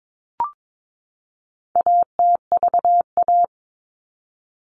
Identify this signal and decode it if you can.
Morse code, heard as a keyed tone, about 22 wpm, 700 Hz. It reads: ATVA
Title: All Cut Numbers - 4-Digit 22wpm